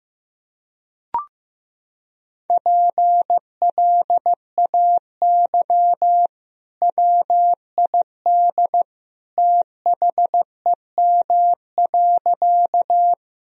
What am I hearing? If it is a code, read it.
Morse code, heard as a keyed tone, about 15 wpm, 700 Hz. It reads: PLAY WID THEM.
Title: Sentences from Top 200 Words and Abbreviations 15wpm